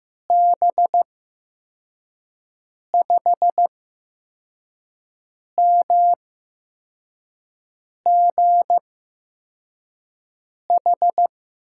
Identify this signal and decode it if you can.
Morse code, heard as a keyed tone, about 15 wpm, spaced as for 4 wpm, 700 Hz. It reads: B5MGH
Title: USA Call Signs - 6 Characters - 8x Character Spacing - 15wpm